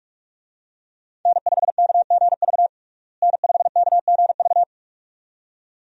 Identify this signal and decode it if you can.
Morse code, heard as a keyed tone, about 43 wpm, 700 Hz. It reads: N5KGV N5KGV
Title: USA Call Signs - 5 Characters - Repeated 2x - 45wpm